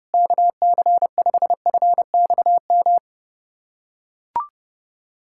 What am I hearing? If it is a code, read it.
Morse code, heard as a keyed tone, about 30 wpm, 700 Hz. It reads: KC5FXM
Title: USA Callsign DE Callsign 30wpm